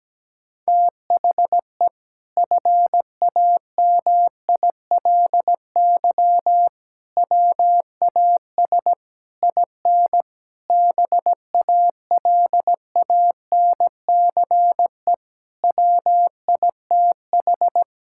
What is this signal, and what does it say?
Morse code, heard as a keyed tone, about 17 wpm, 700 Hz. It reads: THE FAMILY WAS IN BALANCE WITH
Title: Sentences from Top 1000 Words - Review Part 6 - 17wpm